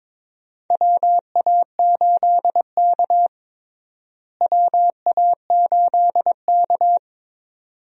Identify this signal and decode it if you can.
Morse code, heard as a keyed tone, about 22 wpm, 700 Hz. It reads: WA8K WA8K
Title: USA Call Signs - 4 Characters - Repeated 2x - 22wpm